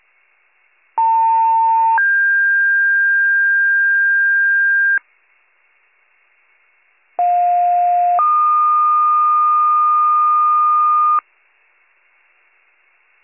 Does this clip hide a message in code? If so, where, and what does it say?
Yes: two-tone page from 1.0 to 11.2 s, heard as a one-second tone, then a three-second tone, about 900 then 1600 Hz, about 710 then 1220 Hz.